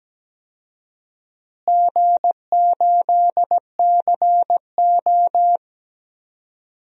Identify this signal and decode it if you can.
Morse code, heard as a keyed tone, about 17 wpm, 700 Hz. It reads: G8CO
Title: DXCC-100 Call Signs - 4 Characters - Repeated 2x - 17wpm